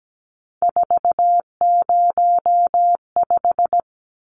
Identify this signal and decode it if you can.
Morse code, heard as a keyed tone, about 17 wpm, 700 Hz. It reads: 405
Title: Numbers - 3-Digit 17wpm